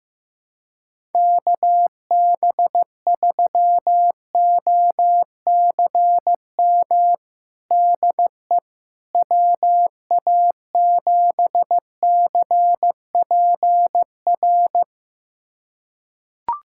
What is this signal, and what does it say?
Morse code, heard as a keyed tone, about 16 wpm, 700 Hz. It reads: KB3OCM DE WA7CPR
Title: USA Callsign DE Callsign 15wpm